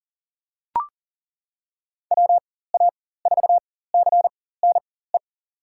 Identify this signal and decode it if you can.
Morse code, heard as a keyed tone, about 40 wpm, spaced as for 19 wpm, 690 Hz. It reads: WA4CNE
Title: USA Call Signs - 4x Character Spacing - 40wpm